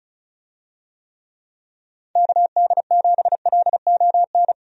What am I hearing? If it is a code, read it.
Morse code, heard as a keyed tone, about 36 wpm, 700 Hz. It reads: KD7LON
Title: USA Call Signs - 6 Characters - 35wpm